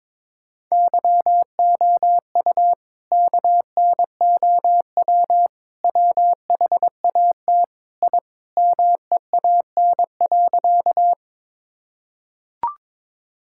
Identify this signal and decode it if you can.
Morse code, heard as a keyed tone, about 21 wpm, 700 Hz. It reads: YOU KNOW WHAT I MEAN.
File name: practice / 22wpm / Sentences with 4-Word Phrases - 22wpm